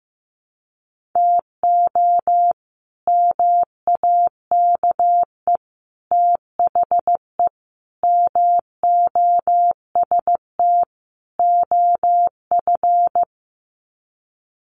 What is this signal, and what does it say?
Morse code, heard as a keyed tone, about 15 wpm, 700 Hz. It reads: TO MAKE THE MOST OF
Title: Sets of 5 Words Encore2 15wpm